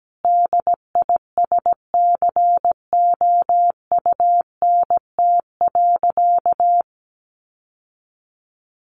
Morse code, heard as a keyed tone, about 17 wpm, 700 Hz. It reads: DISCOUNT.
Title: Sentences from Top 1200 Words 17wpm